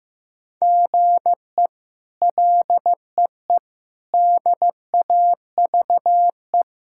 Morse code, heard as a keyed tone, about 15 wpm, 700 Hz. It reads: GE LEE DAVE
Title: K1USN SST QSO Practice - 15wpm